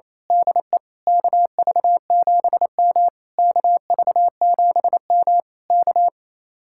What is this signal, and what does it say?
Morse code, heard as a keyed tone, about 26 wpm, 700 Hz. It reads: DE KV7M KV7M K
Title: Speed-Racing Common QSO Phrases Encore - 28wpm